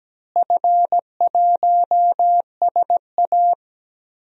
Morse code, heard as a keyed tone, about 17 wpm, 700 Hz. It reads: F1SA
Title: DXCC-100 Call Signs - 4 Characters - Repeated 3x - 17wpm